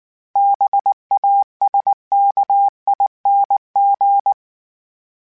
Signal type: Morse code